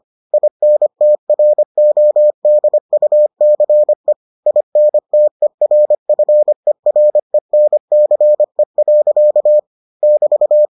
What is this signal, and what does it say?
Morse code, heard as a keyed tone, about 25 wpm, 600 Hz. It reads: INTRODUCE INTERFERENCE. <BT>